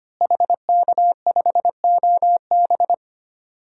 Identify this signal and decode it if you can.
Morse code, heard as a keyed tone, about 25 wpm, 700 Hz. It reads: HK5OB
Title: DXCC-100 Call Signs - 5 Characters - Repeated 2x - 25wpm